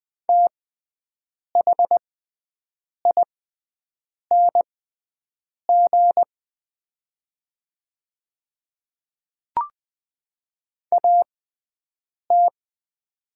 Morse code, heard as a keyed tone, about 20 wpm, spaced as for 7 wpm, 700 Hz. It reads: THING AT
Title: 6x Character Spacing - Top 100 Words - 20wpm